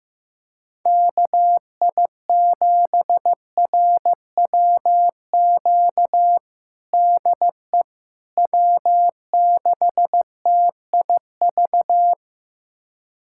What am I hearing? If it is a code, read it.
Morse code, heard as a keyed tone, about 15 wpm, 700 Hz. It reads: KI7RWQ DE W6TIV
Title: USA Callsign DE Callsign 15wpm